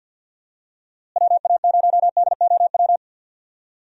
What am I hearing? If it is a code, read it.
Morse code, heard as a keyed tone, about 51 wpm, 700 Hz. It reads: WA0DOW
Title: USA Call Signs - 6 Characters - Repeated 2x - 50wpm